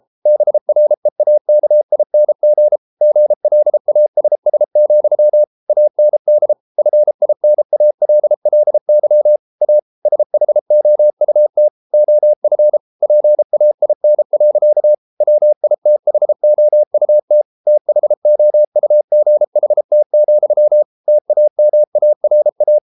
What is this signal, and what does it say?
Morse code, heard as a keyed tone, about 33 wpm, 600 Hz. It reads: BREAKING GLASS, AND FINALLY A SHOUT OF PAIN. WITHOUT THOUGHT, TAMARA